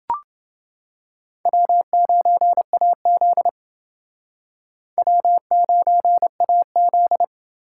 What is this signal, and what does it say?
Morse code, heard as a keyed tone, about 28 wpm, 700 Hz. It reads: W9AZ W9AZ